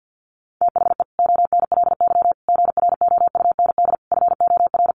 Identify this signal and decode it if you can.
Morse code, heard as a keyed tone, about 50 wpm, 700 Hz. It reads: THE ONLY GROUND FOR
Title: Sentences from Top 1400 Words 50wpm